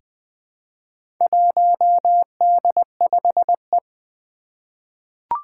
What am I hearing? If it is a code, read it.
Morse code, heard as a keyed tone, about 20 wpm, 700 Hz. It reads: 1D5E
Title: Four Challenging Letter-Number Combinations 20wpm